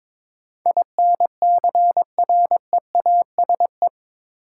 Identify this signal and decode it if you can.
Morse code, heard as a keyed tone, about 21 wpm, 700 Hz. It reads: INCREASE
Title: Sets of 5 Words - New Word Drill - 22wpm